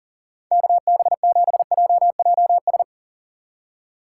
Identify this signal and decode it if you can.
Morse code, heard as a keyed tone, about 42 wpm, 700 Hz. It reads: KB7JJS
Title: Speed Racing - USA 2x3 Call Signs 40wpm